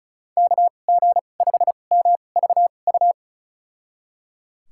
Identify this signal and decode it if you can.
Morse code, heard as a keyed tone, about 36 wpm, spaced as for 25 wpm, 700 Hz. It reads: KG5MVU
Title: USA Call Signs - 6 Characters - 2x Character Spacing - 35wpm